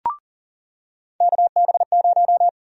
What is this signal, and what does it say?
Morse code, heard as a keyed tone, about 40 wpm, 700 Hz. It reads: KB0
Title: USA Call Sign Prefixes - 40wpm